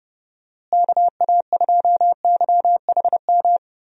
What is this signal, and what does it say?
Morse code, heard as a keyed tone, about 30 wpm, 700 Hz. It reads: KA2YHM
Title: USA Call Signs - 6 Characters - Repeated 3x - 30wpm